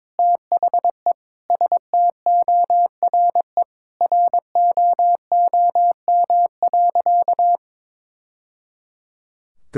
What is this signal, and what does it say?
Morse code, heard as a keyed tone, about 22 wpm, 700 Hz. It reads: THE STORE ROOM.
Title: Sentences from Top 1000 Words - Review Part 5 - 22wpm